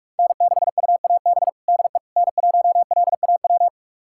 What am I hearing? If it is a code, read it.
Morse code, heard as a keyed tone, about 47 wpm, 700 Hz. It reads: N6UAB DE N1LAW
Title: USA Callsign DE Callsign - 5 Characters - 45wpm